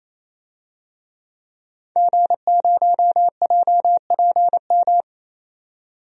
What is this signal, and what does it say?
Morse code, heard as a keyed tone, about 28 wpm, 700 Hz. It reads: G0JPM
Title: DXCC-100 Call Signs - 5 Characters - 28wpm